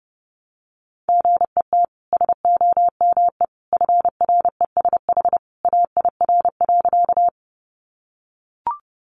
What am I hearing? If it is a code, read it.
Morse code, heard as a keyed tone, about 30 wpm, 700 Hz. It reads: GET SOME FRESH AIR.
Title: Sentences from Top 1200 Words 30wpm